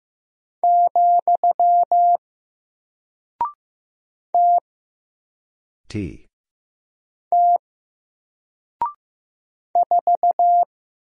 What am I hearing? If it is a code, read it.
Morse code, heard as a keyed tone, about 15 wpm, 700 Hz. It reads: , T T 4
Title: Single Character (Letter:Number:Punctuation:Prosign) 15wpm